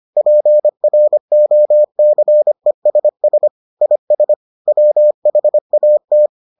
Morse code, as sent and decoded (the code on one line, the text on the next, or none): .--. .-. --- -.-. ... ... / .. ... / .-- .... .- -
PROCESS IS WHAT